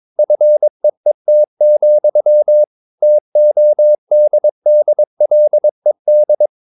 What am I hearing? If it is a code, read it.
Morse code, heard as a keyed tone, about 22 wpm, 600 Hz. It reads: FEET, TODDLED
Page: Romanen Never af Ken Follet på engelsk.